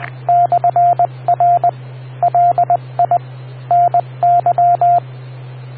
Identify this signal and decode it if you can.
Morse code, heard as a keyed tone, about 20 wpm, 700 Hz, with radio noise. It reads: /R LI NY